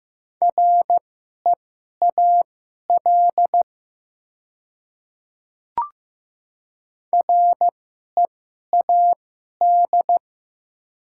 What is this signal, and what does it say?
Morse code, heard as a keyed tone, about 15 wpm, spaced as for 11 wpm, 700 Hz. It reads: REAL READ
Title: 2x Character Spacing - Top 300 Words - 15wpm